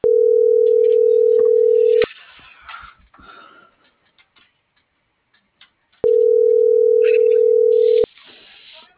Ringback tone (North America), 440 + 480 Hz.